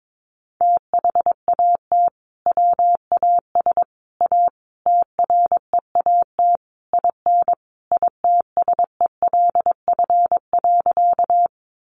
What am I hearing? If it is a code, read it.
Morse code, heard as a keyed tone, about 22 wpm, 700 Hz. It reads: THAT WAS A TREAT IN ITSELF.
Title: Sentences from Top 1300 Words 22wpm